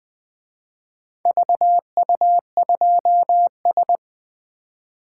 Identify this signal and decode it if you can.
Morse code, heard as a keyed tone, about 24 wpm, 700 Hz. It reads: VU2S
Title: Speed-Racing - Four Letter-Number Combinations - 20wpm